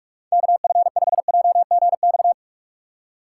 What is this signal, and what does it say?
Morse code, heard as a keyed tone, about 43 wpm, 700 Hz. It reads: KF5JGX
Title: USA Call Signs - 6 Characters - 4x Character Spacing - 45wpm